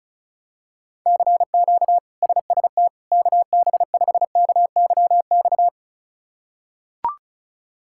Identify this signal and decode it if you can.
Morse code, heard as a keyed tone, about 34 wpm, 700 Hz. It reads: CQ SST KB5KYX
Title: K1USN SST QSO Practice - 35wpm